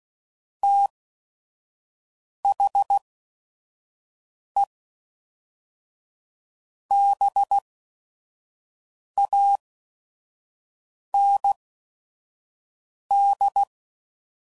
Morse code, heard as a keyed tone, about 16 wpm, spaced as for 5 wpm, 800 Hz. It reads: THE BAND
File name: CW Code Groups @5WPM / QSOs / WD0FBT 1